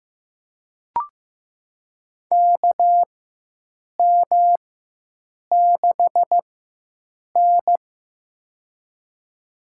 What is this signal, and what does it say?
Morse code, heard as a keyed tone, about 15 wpm, spaced as for 7 wpm, 700 Hz. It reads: KM6N
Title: USA Call Signs - 4x Character Spacing - 15wpm